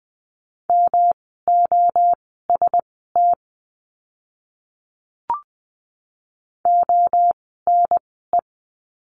Morse code, heard as a keyed tone, about 20 wpm, spaced as for 14 wpm, 700 Hz. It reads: MOST ONE
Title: 115 Lesson 19f - Letter w - Words 20wpm